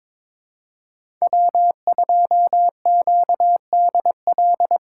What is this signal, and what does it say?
Morse code, heard as a keyed tone, about 22 wpm, 700 Hz. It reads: W2QDL
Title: USA Call Signs - 5 Characters - 6x Character Spacing - 22wpm